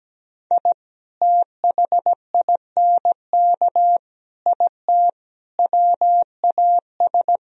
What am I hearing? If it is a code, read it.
Morse code, heard as a keyed tone, about 17 wpm, 700 Hz. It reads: I THINK IT WAS
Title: Speed-Racing - Sets of 4 Words 17wpm